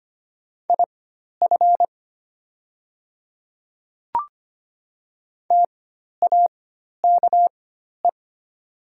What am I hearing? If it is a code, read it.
Morse code, heard as a keyed tone, about 25 wpm, spaced as for 12 wpm, 700 Hz. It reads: IF TAKE
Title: 4x Character Spacing - Top 300 Words - 25wpm